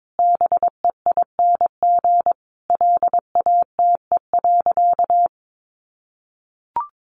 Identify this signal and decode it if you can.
Morse code, heard as a keyed tone, about 22 wpm, 700 Hz. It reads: BEING LATE.
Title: Sentences from Top 1500 Words 22wpm